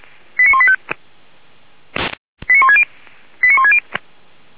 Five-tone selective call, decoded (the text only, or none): 89185 89160 89160